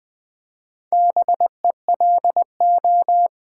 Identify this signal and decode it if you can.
Morse code, heard as a keyed tone, about 20 wpm, 700 Hz. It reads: BELO
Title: Build A Word - 2 Letters At A Time - 6-Letter Words - 20wpm